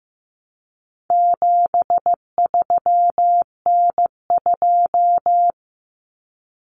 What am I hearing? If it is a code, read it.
Morse code, heard as a keyed tone, about 15 wpm, 700 Hz. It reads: 73N2
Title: Common Cut Numbers - 4-Digit 15wpm